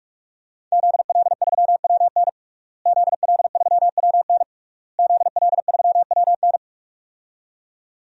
Morse code, heard as a keyed tone, about 41 wpm, 700 Hz. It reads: ZL3WN ZL3WN ZL3WN
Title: DXCC-100 Call Signs - 5 Characters - Repeated 3x - 45wpm